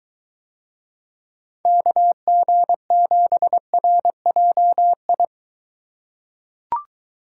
Morse code, heard as a keyed tone, about 21 wpm, 700 Hz. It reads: KG7RJI